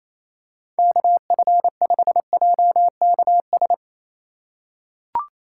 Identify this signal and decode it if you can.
Morse code, heard as a keyed tone, about 28 wpm, 700 Hz. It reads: KF5JKS
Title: USA Call Signs - 6 Characters - 4x Character Spacing - 28wpm